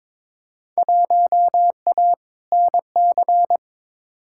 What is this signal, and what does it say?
Morse code, heard as a keyed tone, about 22 wpm, 700 Hz. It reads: 1A NC
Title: Field Day Exchanges Encore - 22wpm